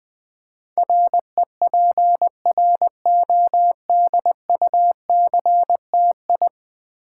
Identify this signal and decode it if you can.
Morse code, heard as a keyed tone, about 20 wpm, 700 Hz. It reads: REPRODUCTI
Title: Build A Word - 5 Letters At A Time - 12-Letter Words - 20wpm